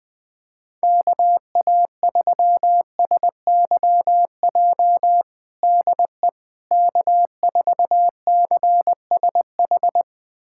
Morse code, heard as a keyed tone, about 20 wpm, 700 Hz. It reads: KA3SYJ DE K4CSH
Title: USA Callsign DE Callsign 20wpm